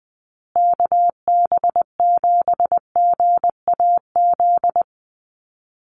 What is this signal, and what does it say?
Morse code, heard as a keyed tone, about 20 wpm, 700 Hz. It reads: KB7GAZ
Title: USA Call Signs Repeated 2x 20wpm